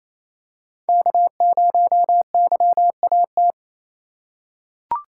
Morse code, heard as a keyed tone, about 28 wpm, 700 Hz. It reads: K0YAT
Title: USA Call Signs - 3x Character Spacing - 28wpm